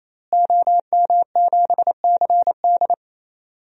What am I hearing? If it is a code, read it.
Morse code, heard as a keyed tone, about 28 wpm, 700 Hz. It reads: OM7CD
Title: DXCC-100 Call Signs - 5 Characters - Repeated 3x - 28wpm